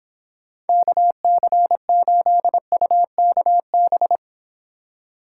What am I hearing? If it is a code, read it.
Morse code, heard as a keyed tone, about 29 wpm, 700 Hz. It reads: KC8UKB